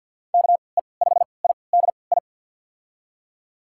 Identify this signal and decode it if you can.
Morse code, heard as a keyed tone, about 47 wpm, spaced as for 27 wpm, 700 Hz. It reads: KE5IDI
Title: USA Call Signs - 6 Characters - 3x Character Spacing - 50wpm